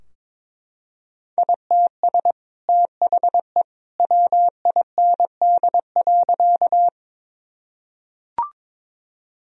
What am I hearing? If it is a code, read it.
Morse code, heard as a keyed tone, about 23 wpm, 700 Hz. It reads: ITS THE WIND.
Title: Sentences from Top 1000 Words - Review Part 2 - 22wpm